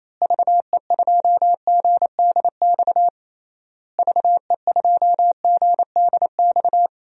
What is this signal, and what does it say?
Morse code, heard as a keyed tone, about 28 wpm, 700 Hz. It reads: VE2GDX VE2GDX
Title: DXCC-100 Call Signs - 6 Characters - Repeated 3x - 28wpm